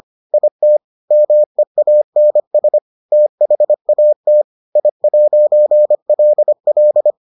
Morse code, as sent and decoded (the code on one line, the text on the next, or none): .. - / -- . .- -. ... / - .... .- - / .. .----. .-.. .-..
IT MEANS THAT I'LL